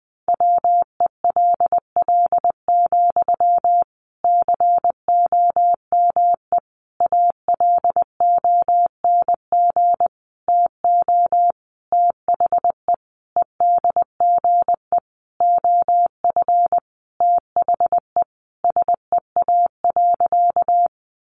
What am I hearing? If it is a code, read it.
Morse code, heard as a keyed tone, about 20 wpm, 700 Hz. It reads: WELL, COME ALONG TO THE EDGE OF THE SEA.
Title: Sentences from Top 1300 Words 20wpm